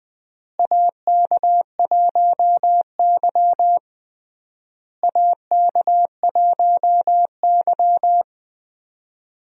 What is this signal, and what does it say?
Morse code, heard as a keyed tone, about 21 wpm, 700 Hz. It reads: AK1Y AK1Y
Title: USA Call Signs - 4 Characters - Repeated 3x - 20wpm